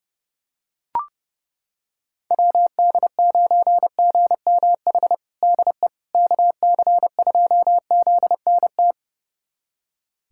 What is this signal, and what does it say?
Morse code, heard as a keyed tone, about 32 wpm, 700 Hz. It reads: WD9GMH DE KC2ZNT
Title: USA Callsign DE Callsign 30wpm